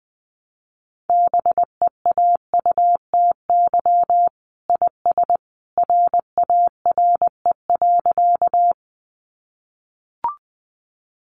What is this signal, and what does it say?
Morse code, heard as a keyed tone, about 20 wpm, 700 Hz. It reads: BEAUTY IS RARE.
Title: Sentences from Top 1100 Words 20wpm